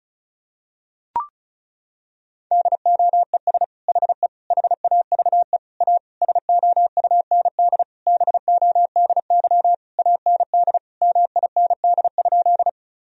Morse code, heard as a keyed tone, about 34 wpm, 700 Hz. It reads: DOES HE HAVE A SOUND BODY AND MIND?
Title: Sentences from Top 1000 Words - Review Part 2 - 35wpm